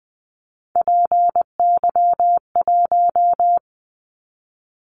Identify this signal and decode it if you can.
Morse code, heard as a keyed tone, about 20 wpm, 700 Hz. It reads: PY1
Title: Callsign Prefixes for Top 50 DXCC - 20wpm